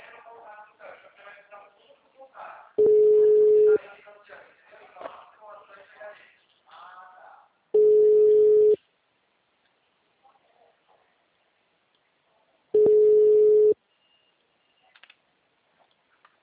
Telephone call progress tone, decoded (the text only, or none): ringback tone